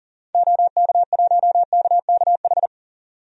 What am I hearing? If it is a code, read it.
Morse code, heard as a keyed tone, about 39 wpm, 700 Hz. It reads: OK1KKH
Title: DXCC-100 Call Signs - 6 Characters - 40wpm